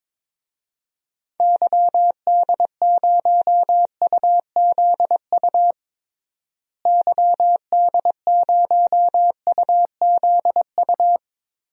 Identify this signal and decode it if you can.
Morse code, heard as a keyed tone, about 21 wpm, 700 Hz. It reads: YD0UZU YD0UZU
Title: DXCC-100 Call Signs - 6 Characters - Repeated 3x - 22wpm